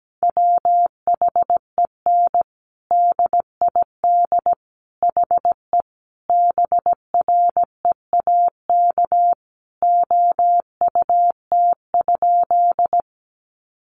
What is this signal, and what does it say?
Morse code, heard as a keyed tone, about 17 wpm, 700 Hz. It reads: WHEN DID HE BREAK OUT?
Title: Sentences from Top 900 Words 17wpm